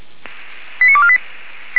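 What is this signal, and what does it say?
Five-tone selective call (ZVEI), heard as quick runs of five tones, tones of about 70 ms. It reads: 89248
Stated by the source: Received by SDR, ZVEI five-tone call 89360.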